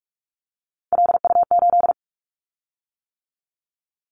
Morse code, heard as a keyed tone, about 46 wpm, 700 Hz. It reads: LU8
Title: Callsign Prefixes for Top 50 DXCC - 45wpm